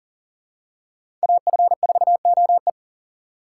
Morse code, heard as a keyed tone, about 37 wpm, 700 Hz. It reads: AF4OE
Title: USA Call Signs - 5 Characters - 4x Character Spacing - 40wpm